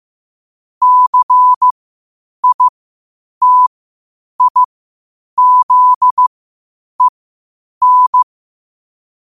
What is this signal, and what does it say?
Morse code, heard as a keyed tone, about 15 wpm, spaced as for 8 wpm, 1000 Hz. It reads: CITIZEN
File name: qotdfiles / 8wpm